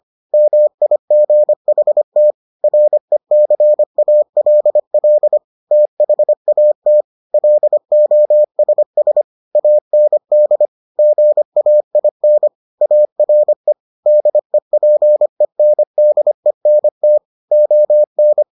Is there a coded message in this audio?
Yes: Morse code from 0.3 to 18.4 s, heard as a keyed tone, about 25 wpm, 600 Hz. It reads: MIGHT RECALL THAT LOSS AND GAIN ARE DEPENDENT ON